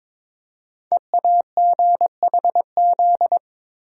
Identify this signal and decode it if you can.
Morse code, heard as a keyed tone, about 22 wpm, 700 Hz. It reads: EAGHZ
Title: Code Groups - 5 Letters - 22wpm